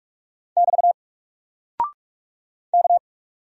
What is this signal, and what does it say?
Morse code, heard as a keyed tone, about 44 wpm, 700 Hz. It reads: <BT> K
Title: Single Character (Letter:Number:Punctuation:Prosign) Alt 45wpm